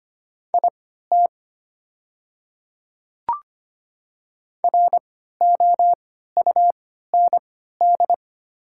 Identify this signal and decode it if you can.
Morse code, heard as a keyed tone, about 25 wpm, spaced as for 14 wpm, 700 Hz. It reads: IT ROUND
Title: 3x Character Spacing - Top 200 Words - 25wpm